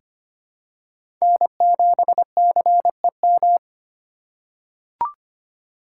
Morse code, heard as a keyed tone, about 25 wpm, 700 Hz. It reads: N7CEM
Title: USA Call Signs - 5 Characters - 3x Character Spacing - 25wpm